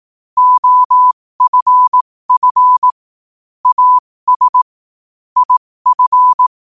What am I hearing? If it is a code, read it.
Morse code, heard as a keyed tone, about 18 wpm, spaced as for 15 wpm, 1000 Hz. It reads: OFF AS IF
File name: qotdfiles / 15wpm